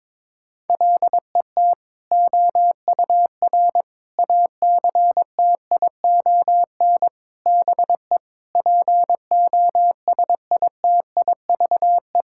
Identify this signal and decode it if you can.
Morse code, heard as a keyed tone, about 22 wpm, 700 Hz. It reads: LET OUR ACTION BE POSITIVE
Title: Sentences from Top 1000 Words - Review Part 6 - 22wpm